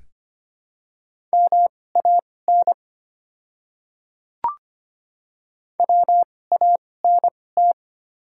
Morse code, heard as a keyed tone, about 25 wpm, spaced as for 18 wpm, 700 Hz. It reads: MAN WANT
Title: 2x Character Spacing - Top 200 Words - 25wpm